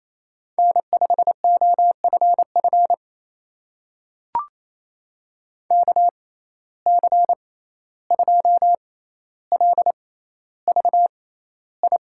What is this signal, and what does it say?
Morse code, heard as a keyed tone, about 28 wpm, spaced as for 14 wpm, 700 Hz. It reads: N5OFF KC2LVI
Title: USA Call Signs - 6x Character Spacing - 28wpm